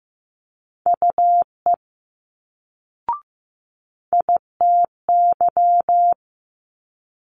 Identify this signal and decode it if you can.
Morse code, heard as a keyed tone, about 15 wpm, 700 Hz. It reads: UE ITY
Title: Multi-letter Phonemes Quick Fire 15wpm